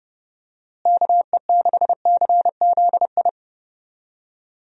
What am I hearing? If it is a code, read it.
Morse code, heard as a keyed tone, about 30 wpm, 700 Hz. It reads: KE6CZI
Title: USA Call Signs - 6 Characters - Repeated 2x - 30wpm